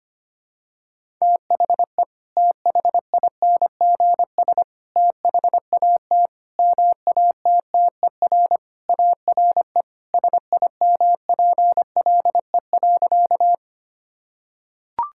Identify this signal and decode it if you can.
Morse code, heard as a keyed tone, about 25 wpm, 700 Hz. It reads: THE THINGS THAT MATTER ARE SIMPLE.
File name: practice / 25wpm / Sentences with 3-Word Phrases - 25wpm